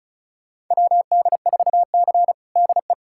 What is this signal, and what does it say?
Morse code, heard as a keyed tone, about 36 wpm, 700 Hz. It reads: WD4C DE
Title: USA Callsign DE Callsign - 4 Characters - 35wpm